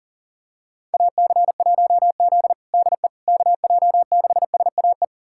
Morse code, heard as a keyed tone, about 40 wpm, 700 Hz. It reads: AC1Z DE KJ6SAE
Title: USA Callsign DE Callsign 40wpm